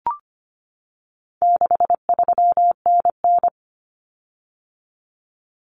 Morse code, heard as a keyed tone, about 25 wpm, 700 Hz. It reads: 63NN
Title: Common Cut Numbers - 4-Digit 25wpm